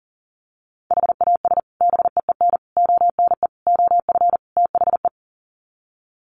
Morse code, heard as a keyed tone, about 40 wpm, 700 Hz. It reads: HAS BEEN ONE OF THE
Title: Sets of 5 Words Encore 40wpm